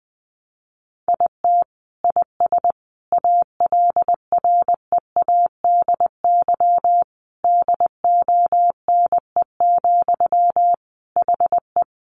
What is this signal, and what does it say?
Morse code, heard as a keyed tone, about 20 wpm, 700 Hz. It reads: IT IS ALREADY DONE, HE
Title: Sentences from Top 700 Words 20wpm